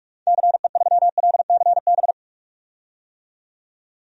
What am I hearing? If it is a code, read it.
Morse code, heard as a keyed tone, about 41 wpm, 700 Hz. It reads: CE3LCB
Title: DXCC-100 Call Signs - 6 Characters - 45wpm